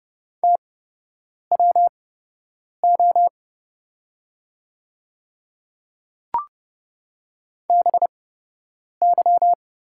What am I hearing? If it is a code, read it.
Morse code, heard as a keyed tone, about 30 wpm, spaced as for 8 wpm, 700 Hz. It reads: TWO BY
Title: 8x Character Spacing - Top 100 Words - 30wpm